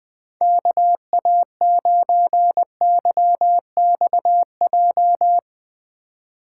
Morse code, heard as a keyed tone, about 19 wpm, 700 Hz. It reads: KA9YXJ
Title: Speed Racing - USA 2x3 Call Signs 17wpm